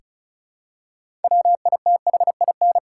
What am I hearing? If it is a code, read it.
Morse code, heard as a keyed tone, about 34 wpm, 700 Hz. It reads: WITHIN
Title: Six Letter Words - 35wpm